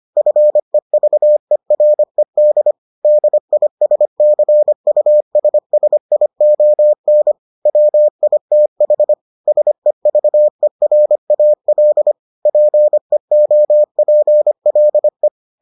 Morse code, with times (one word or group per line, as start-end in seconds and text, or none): 0.2-2.7 FEVERED
3.0-7.3 DISCUSSION
7.7-9.1 WITH
9.5-12.1 SEVERAL
12.5-15.3 PEOPLE